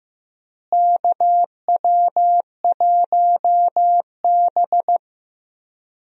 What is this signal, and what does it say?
Morse code, heard as a keyed tone, about 15 wpm, 700 Hz. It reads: KW1B
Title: USA Call Signs - 4 Characters - Repeated 3x - 15wpm